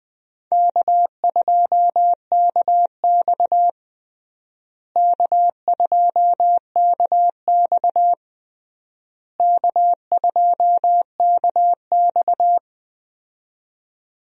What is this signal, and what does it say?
Morse code, heard as a keyed tone, about 20 wpm, 700 Hz. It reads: K2KX K2KX K2KX
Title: USA Call Signs - 4 Characters - Repeated 3x - 20wpm